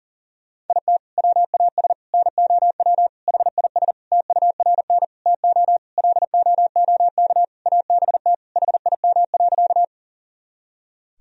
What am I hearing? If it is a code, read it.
Morse code, heard as a keyed tone, about 40 wpm, 700 Hz. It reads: IT WAS NOW HIS TURN TO LOOK ABT HIM.
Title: Sentences from Top 200 Words and Abbreviations 40wpm